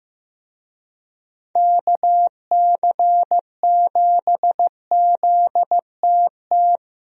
Morse code, heard as a keyed tone, about 15 wpm, 700 Hz. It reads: KC7ZTT
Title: USA Call Signs - 6 Characters - Repeated 3x - 15wpm